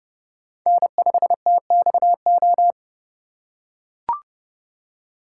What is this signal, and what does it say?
Morse code, heard as a keyed tone, about 30 wpm, 700 Hz. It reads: N5TXO
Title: USA Call Signs - 5 Characters - 2x Character Spacing - 30wpm